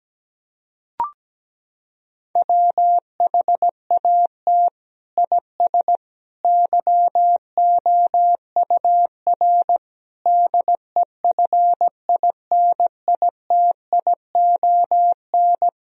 Morse code, heard as a keyed tone, about 17 wpm, 700 Hz. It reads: WHAT IS YOUR DEFINITION
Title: Sentences from Top 1000 Words - Review Part 4 - 17wpm